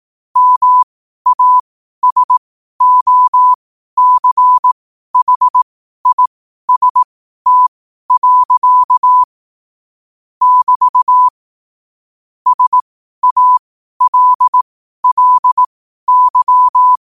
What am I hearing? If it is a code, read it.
Morse code, heard as a keyed tone, about 18 wpm, spaced as for 12 wpm, 1000 Hz. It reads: MASOCHIST. <BT> SALLY
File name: qotdfiles / 12wpm